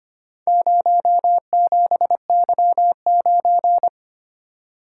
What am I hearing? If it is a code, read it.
Morse code, heard as a keyed tone, about 25 wpm, 700 Hz. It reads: 07Y9
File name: practice / 25wpm / Four Letter-Number Combinations - 25wpm